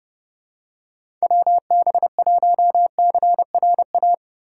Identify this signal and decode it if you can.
Morse code, heard as a keyed tone, about 32 wpm, 700 Hz. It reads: WB1CRA
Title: USA Call Signs - 6 Characters - 3x Character Spacing - 30wpm